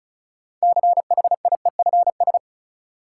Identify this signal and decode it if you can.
Morse code, heard as a keyed tone, about 36 wpm, 700 Hz. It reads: CHIEFS
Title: Sets of 5 Words Encore - New Word Drill - 35wpm